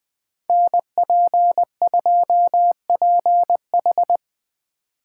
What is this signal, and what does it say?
Morse code, heard as a keyed tone, about 20 wpm, 700 Hz. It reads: NP2PH
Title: Speed Racing - USA 2x2 Call Signs 17wpm